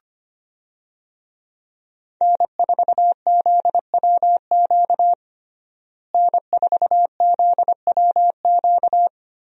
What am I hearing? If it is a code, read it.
Morse code, heard as a keyed tone, about 25 wpm, 700 Hz. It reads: N4ZWQ N4ZWQ
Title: USA Call Signs - 5 Characters - Repeated 3x - 25wpm